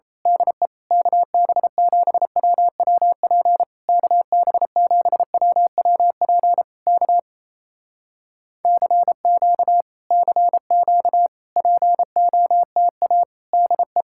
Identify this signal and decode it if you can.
Morse code, heard as a keyed tone, about 27 wpm, 700 Hz. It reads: DE KB7WWP KB7WWP K CQ CQ POTA DE
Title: Speed-Racing Common QSO Phrases Encore - 28wpm